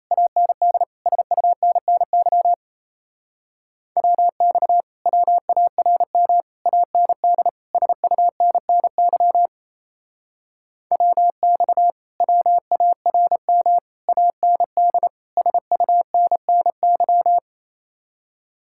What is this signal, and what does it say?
Morse code, heard as a keyed tone, about 35 wpm, 700 Hz. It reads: AND SUNNY WX WARM AND SUNNY WX WARM AND SUNNY